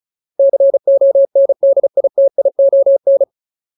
Morse code, heard as a keyed tone, about 34 wpm, 550 Hz. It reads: CONDITION
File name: Ultimate Word List / 35WPM Ultimate